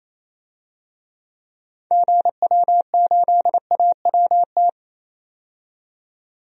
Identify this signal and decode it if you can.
Morse code, heard as a keyed tone, about 27 wpm, 700 Hz. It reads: GW8AWT
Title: DXCC-100 Call Signs - 6 Characters - 28wpm